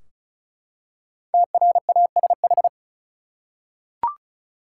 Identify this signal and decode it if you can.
Morse code, heard as a keyed tone, about 34 wpm, 700 Hz. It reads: TRASH
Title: Five Letter Words - 35wpm